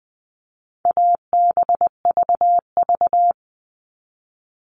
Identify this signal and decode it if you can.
Morse code, heard as a keyed tone, about 20 wpm, 700 Hz. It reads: ABVV
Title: All Cut Numbers - 4-Digit 20wpm